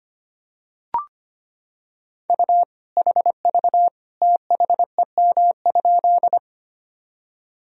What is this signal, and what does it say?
Morse code, heard as a keyed tone, about 25 wpm, 700 Hz. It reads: U HV THEM?
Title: Sentences from Top 100 Words and Abbreviations 25wpm